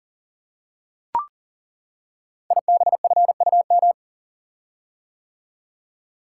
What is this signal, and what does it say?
Morse code, heard as a keyed tone, about 38 wpm, 700 Hz. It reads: IBFUM
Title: Code Groups - 5 Letter:Numbers - 40wpm